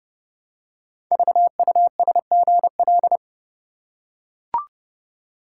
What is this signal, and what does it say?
Morse code, heard as a keyed tone, about 29 wpm, 700 Hz. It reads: VUSGL
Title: Code Groups - 5 Letters - 30wpm